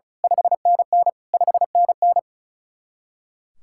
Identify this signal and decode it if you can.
Morse code, heard as a keyed tone, about 32 wpm, 700 Hz. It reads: 5NN 5NN
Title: Speed-Racing Common QSO Phrases Encore - 35wpm